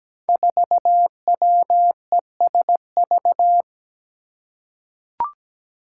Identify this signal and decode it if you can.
Morse code, heard as a keyed tone, about 17 wpm, 700 Hz. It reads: 4WESV
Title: Five Challenging Letter-Number Combinations 17wpm